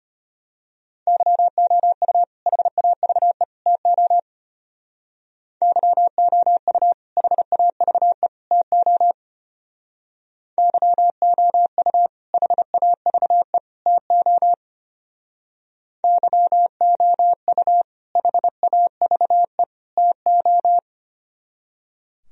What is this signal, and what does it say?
Morse code, heard as a keyed tone, about 29 wpm, 700 Hz. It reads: YOU HAVE TO YOU HAVE TO YOU HAVE TO YOU HAVE TO
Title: Speed-Racing - Sets of 3 Words 25wpm